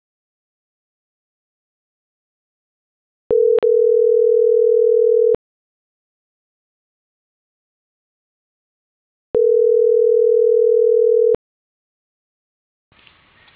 Ringback tone (North America), 440 + 480 Hz.